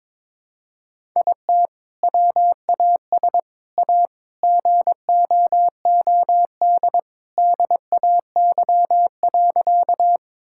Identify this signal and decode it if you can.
Morse code, heard as a keyed tone, about 21 wpm, 700 Hz. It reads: IT WAS A GOOD DAY.
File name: practice / 22wpm / Sentences with 2-Word Phrases - 22wpm